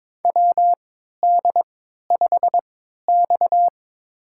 Morse code, heard as a keyed tone, about 21 wpm, spaced as for 12 wpm, 700 Hz. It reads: WD5X
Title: USA Call Signs - 4 Characters - 3x Character Spacing - 22wpm